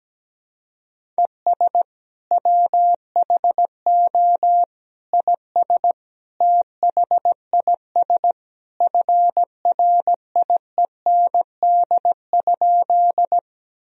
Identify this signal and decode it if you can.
Morse code, heard as a keyed tone, about 17 wpm, 700 Hz. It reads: ES WHO IS THIS FRIEND?
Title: Sentences from Top 300 Words and Abbreviations 17wpm